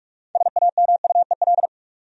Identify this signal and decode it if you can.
Morse code, heard as a keyed tone, about 46 wpm, 700 Hz. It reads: SAMUEL
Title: USA Common Names - 45wpm